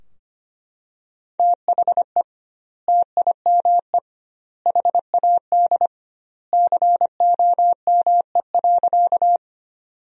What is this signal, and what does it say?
Morse code, heard as a keyed tone, about 25 wpm, 700 Hz. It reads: THE TIME HAD COME.
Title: 2x Word-Spacing - Sentences from Top 100 Words - 25wpm